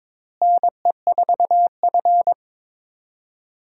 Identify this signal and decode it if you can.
Morse code, heard as a keyed tone, about 22 wpm, 700 Hz. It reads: NE4F
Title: USA Call Signs - 4 Characters - Repeated 2x - 22wpm